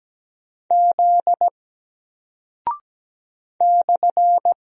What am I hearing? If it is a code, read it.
Morse code, heard as a keyed tone, about 17 wpm, 700 Hz. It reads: Z /
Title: Single Character (Letter:Number:Punctuation:Prosign) Alt 17wpm